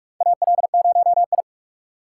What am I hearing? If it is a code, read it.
Morse code, heard as a keyed tone, about 47 wpm, 700 Hz. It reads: AL0I
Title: USA Call Signs - 4x Character Spacing - 45wpm